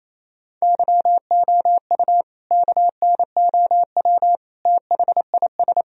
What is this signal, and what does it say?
Morse code, heard as a keyed tone, about 28 wpm, 700 Hz. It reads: YOU KNOW THIS
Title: Sentences from Top 1000 Words - Review Part 2 - 28wpm